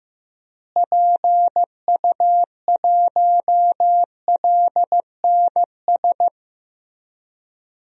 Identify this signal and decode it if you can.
Morse code, heard as a keyed tone, about 15 wpm, 700 Hz. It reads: PU1LNS
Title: DXCC-100 Call Signs - 6 Characters - Repeated 3x - 15wpm